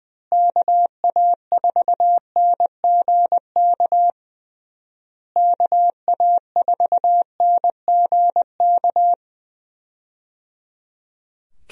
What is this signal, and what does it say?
Morse code, heard as a keyed tone, about 20 wpm, 700 Hz. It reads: KA4NGK KA4NGK
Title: USA Call Signs - 6 Characters - Repeated 3x - 20wpm